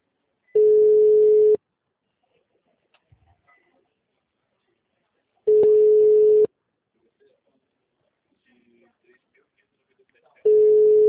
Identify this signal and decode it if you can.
Ringback tone (Europe, 425 Hz), with speech.